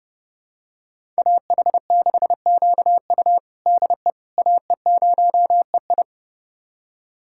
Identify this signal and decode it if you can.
Morse code, heard as a keyed tone, about 32 wpm, 700 Hz. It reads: AH6QU DE AE0EI
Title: USA Callsign DE Callsign - 5 Characters - 30wpm